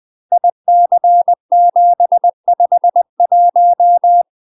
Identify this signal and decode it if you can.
Morse code, heard as a keyed tone, about 20 wpm, 700 Hz. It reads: IC751
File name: UK MASTER QSOs TRAINING / 20WPM / UK MASTER QSOs 0065